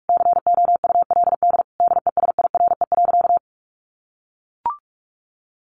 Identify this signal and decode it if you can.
Morse code, heard as a keyed tone, about 44 wpm, 700 Hz. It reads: COULD DESIRE.
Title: Sentences from Top 1100 Words 45wpm